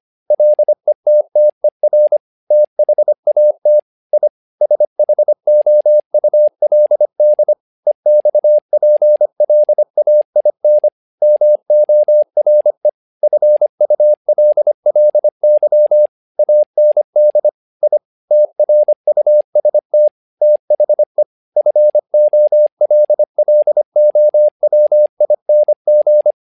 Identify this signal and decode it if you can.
Morse code, heard as a keyed tone, about 25 wpm, 600 Hz. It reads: LETTER THAT I SHOULD EXPLAIN MORE FULLY AND I TRUST THE FOLLOWING
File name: learning center / tower trouble / tower trouble 25wpm